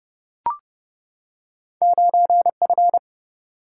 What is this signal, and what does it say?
Morse code, heard as a keyed tone, about 31 wpm, 700 Hz. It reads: 9F
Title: Single-Letter Single-Number 30wpm